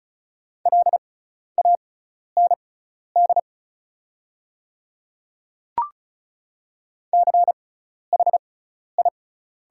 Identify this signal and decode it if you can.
Morse code, heard as a keyed tone, about 37 wpm, spaced as for 12 wpm, 690 Hz. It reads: LAND CHI
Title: 6x Character Spacing - Top 300 Words - 35wpm